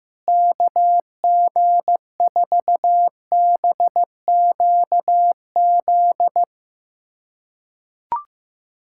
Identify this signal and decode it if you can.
Morse code, heard as a keyed tone, about 15 wpm, 700 Hz. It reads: KG4BQZ
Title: USA Callsign DE Callsign 15wpm